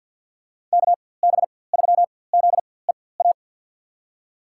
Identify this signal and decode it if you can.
Morse code, heard as a keyed tone, about 51 wpm, spaced as for 23 wpm, 700 Hz. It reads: KB3ZEA
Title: USA Call Signs - 6 Characters - 4x Character Spacing - 50wpm